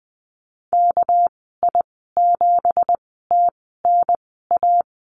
Morse code, heard as a keyed tone, about 20 wpm, spaced as for 15 wpm, 700 Hz. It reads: KI7TNA
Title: 219 Lesson 35g - Number 0 - Call Signs 20wpm